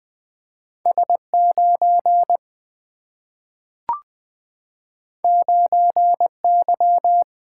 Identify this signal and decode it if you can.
Morse code, heard as a keyed tone, about 20 wpm, 700 Hz. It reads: S9 9Y
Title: Single-Letter Single-Number 20wpm